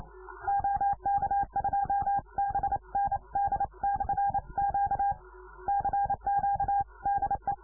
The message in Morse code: OK2BND/Q CQ DE

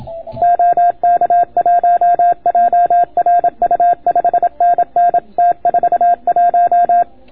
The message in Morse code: OK1JRU 5NN T41

